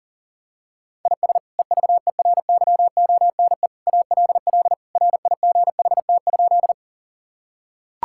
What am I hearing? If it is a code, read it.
Morse code, heard as a keyed tone, about 40 wpm, 700 Hz. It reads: IS EVERYONE ALL RIGHT?
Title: Sentences from Top 1000 Words - Review Part 5 - 40wpm